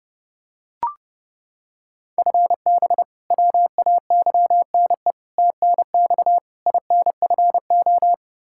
Morse code, heard as a keyed tone, about 30 wpm, 700 Hz. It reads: FB WAYNE TNX INFO
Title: Practice QSO - Exchange 3 - More Info - 30wpm